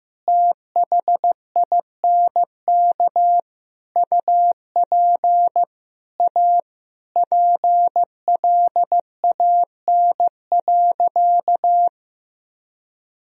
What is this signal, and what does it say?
Morse code, heard as a keyed tone, about 15 wpm, 700 Hz. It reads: THINK UP A PLAN.
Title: Sentences from Top 1000 Words - Review Part 2 - 15wpm